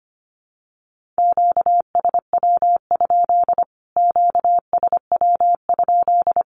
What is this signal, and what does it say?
Morse code, heard as a keyed tone, about 25 wpm, 700 Hz. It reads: QSW? QSW?
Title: Q-Codes Repeated 3x 25wpm